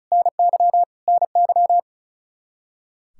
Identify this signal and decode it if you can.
Morse code, heard as a keyed tone, about 36 wpm, 700 Hz. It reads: NY NY
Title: Practice QSO - Exchange 1 - Beginning - 35wpm